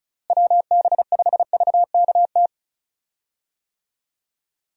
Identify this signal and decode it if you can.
Morse code, heard as a keyed tone, about 36 wpm, 700 Hz. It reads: WB5VKT
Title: USA Call Signs - 6 Characters - Repeated 3x - 35wpm